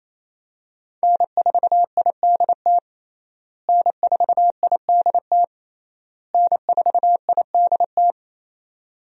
Morse code, heard as a keyed tone, about 28 wpm, 700 Hz. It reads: N4IDT N4IDT N4IDT
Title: USA Call Signs - 5 Characters - Repeated 3x - 28wpm